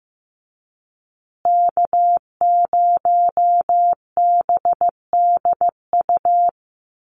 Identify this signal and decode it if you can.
Morse code, heard as a keyed tone, about 15 wpm, 700 Hz. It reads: K0BDU
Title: USA Call Signs Repeated 3x 15wpm